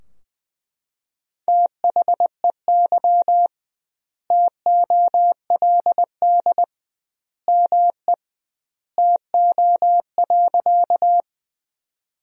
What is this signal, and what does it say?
Morse code, heard as a keyed tone, about 20 wpm, 700 Hz. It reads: THEY TOLD ME TO.
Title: 2x Word-Spacing - Sentences from Top 400 Words - 20wpm